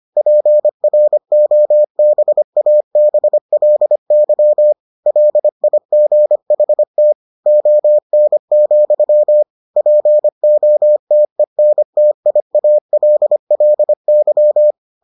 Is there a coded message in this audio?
Yes: Morse code from 0.2 to 14.7 s, heard as a keyed tone, about 25 wpm, 600 Hz. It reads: PROBABLY LIGHT ON, POTENTIALLY